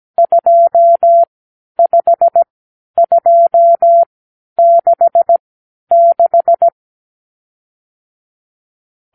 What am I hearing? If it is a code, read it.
Morse code, heard as a keyed tone, about 17 wpm, spaced as for 10 wpm, 680 Hz. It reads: 25266
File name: arrl / Review (1-0)